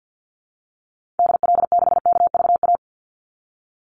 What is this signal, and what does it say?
Morse code, heard as a keyed tone, about 48 wpm, 700 Hz. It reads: DL6KVA
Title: Call Signs Repeated 2x - 50wpm